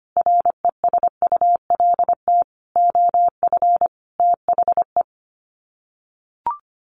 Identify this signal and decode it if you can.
Morse code, heard as a keyed tone, about 25 wpm, 700 Hz. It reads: RESULT OF THE
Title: Sets of 4 Words 25wpm